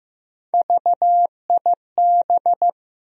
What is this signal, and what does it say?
Morse code, heard as a keyed tone, about 15 wpm, 700 Hz. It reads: VIB
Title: USA Call Sign Postfixes - 3 Letters 15wpm